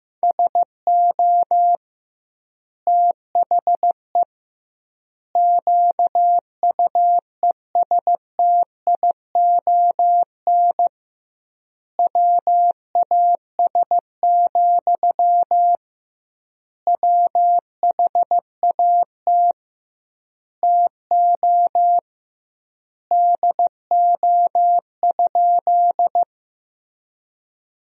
Morse code, heard as a keyed tone, about 15 wpm, 700 Hz. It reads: SO THE QUESTION WAS, WHAT TO DO?
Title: 2x Word-Spacing - Sentences from Top 400 Words - 15wpm